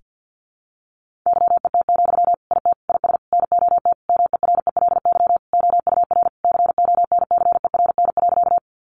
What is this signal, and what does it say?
Morse code, heard as a keyed tone, about 50 wpm, 700 Hz. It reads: YET, IT IS NOT MERELY OUR CONCERN.
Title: Sentences from Top 1400 Words 50wpm